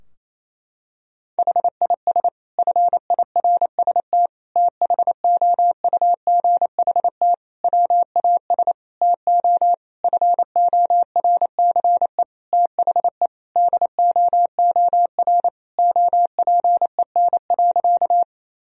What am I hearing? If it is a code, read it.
Morse code, heard as a keyed tone, about 28 wpm, 700 Hz. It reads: HIS FIRST THOUGHT WAS TO FORCE THE DOOR OPEN.
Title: Sentences from Top 1000 Words - Review Part 3 - 28wpm